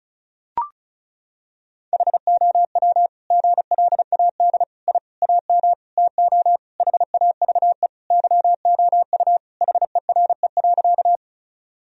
Morse code, heard as a keyed tone, about 36 wpm, 700 Hz. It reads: HOW GLAD I AM TO HAVE YOU HERE.
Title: Sentences from Top 1000 Words - Review Part 2 - 35wpm